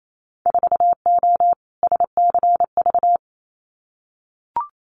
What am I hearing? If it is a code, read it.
Morse code, heard as a keyed tone, about 28 wpm, 700 Hz. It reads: 4O SCV
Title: Winter Field Day Exchanges 28wpm